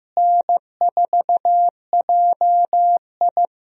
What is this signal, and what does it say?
Morse code, heard as a keyed tone, about 15 wpm, 700 Hz. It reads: N4JI
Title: USA Call Signs - 4 Characters - Repeated 2x - 15wpm